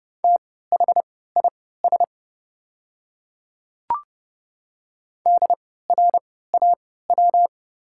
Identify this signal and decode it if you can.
Morse code, heard as a keyed tone, about 30 wpm, spaced as for 17 wpm, 690 Hz. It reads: THIS DRAW
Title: 3x Character Spacing - Top 300 Words - 30wpm